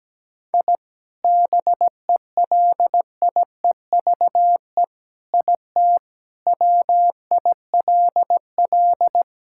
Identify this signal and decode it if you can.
Morse code, heard as a keyed tone, about 17 wpm, 700 Hz. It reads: I BELIEVE IT WILL
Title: Sentences from Top 1000 Words - Review Part 5 - 17wpm